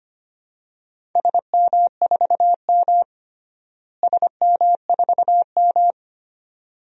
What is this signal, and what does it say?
Morse code, heard as a keyed tone, about 25 wpm, 700 Hz. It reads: SM4M SM4M
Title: DXCC-100 Call Signs - 4 Characters - Repeated 2x - 25wpm